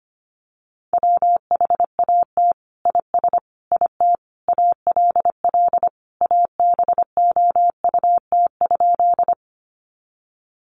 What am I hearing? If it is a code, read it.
Morse code, heard as a keyed tone, about 25 wpm, 700 Hz. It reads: WHAT IS IT ALL ABOUT?
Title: Sentences from Top 100 Words 25wpm